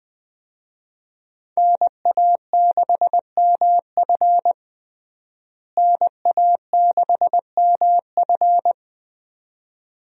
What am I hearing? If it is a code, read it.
Morse code, heard as a keyed tone, about 21 wpm, 700 Hz. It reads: NA6MF NA6MF
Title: USA Call Signs - 5 Characters - Repeated 2x - 20wpm